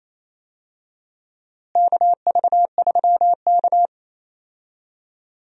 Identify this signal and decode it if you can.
Morse code, heard as a keyed tone, about 28 wpm, 700 Hz. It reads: KV3K
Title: USA Call Signs - 4 Characters - Repeated 2x - 28wpm